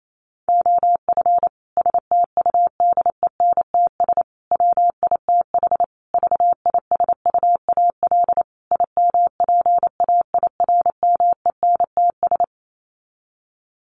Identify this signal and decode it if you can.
Morse code, heard as a keyed tone, about 28 wpm, 700 Hz. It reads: OF STUDENTS WITH VISUAL IMPAIRMENTS
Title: Sets of 5 Words Encore 28wpm